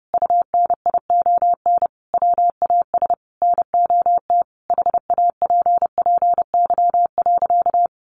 Morse code, heard as a keyed tone, about 30 wpm, 700 Hz. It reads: UNION WAS NOT HAPPY.
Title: Sentences from Top 1400 Words 30wpm